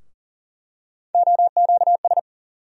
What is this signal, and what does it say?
Morse code, heard as a keyed tone, about 40 wpm, 700 Hz. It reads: OQS
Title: USA Call Sign Postfixes - 3 Letters 40wpm